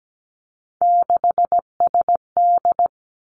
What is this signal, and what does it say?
Morse code, heard as a keyed tone, about 17 wpm, 700 Hz. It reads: 6SD
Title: Three Challenging Letter-Number Combinations 17wpm